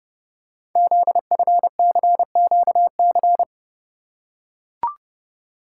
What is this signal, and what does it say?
Morse code, heard as a keyed tone, about 30 wpm, 700 Hz. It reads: ZFCQC
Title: Code Groups - 5 Letters - 30wpm